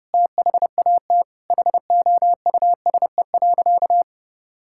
Morse code, heard as a keyed tone, about 32 wpm, 700 Hz. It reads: THAT HOUSE.
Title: Sentences from Top 1000 Words - Review Part 5 - 30wpm